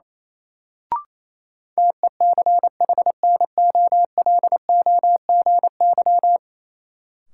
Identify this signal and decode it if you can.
Morse code, heard as a keyed tone, about 28 wpm, 700 Hz. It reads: TECHNOLOGY
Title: Sentences from Top 1000 Words - Review Part 5 - New Word Drill - 28wpm